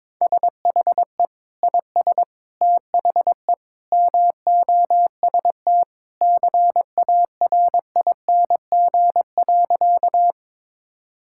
Morse code, heard as a keyed tone, about 22 wpm, 700 Hz. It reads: SHE IS THE MOST CARING.
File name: practice / 22wpm / Sentences with 2-Word Phrases - 22wpm